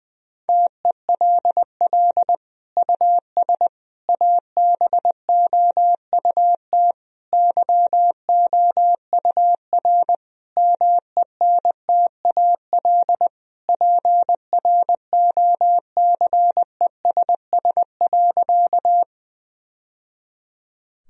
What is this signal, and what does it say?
Morse code, heard as a keyed tone, about 20 wpm, 700 Hz. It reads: TELL US ABOUT YOUR MENTAL PROCESS.
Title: Sentences from Top 1000 Words - Review Part 4 - 20wpm